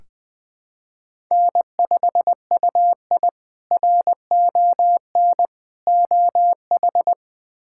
Morse code, heard as a keyed tone, about 20 wpm, 700 Hz. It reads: N5UI RON OH
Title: K1USN SST QSO Practice - 20wpm